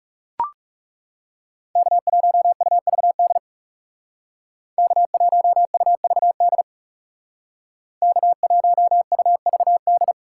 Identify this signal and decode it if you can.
Morse code, heard as a keyed tone, about 39 wpm, 700 Hz. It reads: K1UVD K1UVD K1UVD